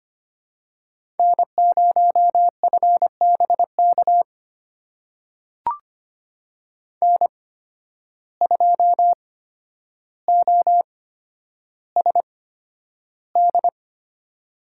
Morse code, heard as a keyed tone, about 25 wpm, spaced as for 11 wpm, 700 Hz. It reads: N0FBK N2OSD
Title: USA Call Signs - 5 Characters - 8x Character Spacing - 25wpm